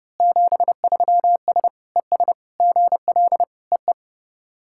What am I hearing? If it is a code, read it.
Morse code, heard as a keyed tone, about 30 wpm, 700 Hz. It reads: 73S ES GL EE
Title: SOTA QSO Practice 30wpm